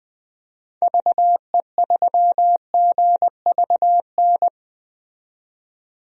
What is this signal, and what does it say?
Morse code, heard as a keyed tone, about 20 wpm, 700 Hz. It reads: VE3GVN
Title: DXCC-100 Call Signs - 6 Characters - 20wpm